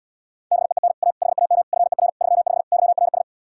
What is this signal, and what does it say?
Morse code, heard as a keyed tone, about 77 wpm, 700 Hz. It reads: BEEN A HAM FER 35 YRS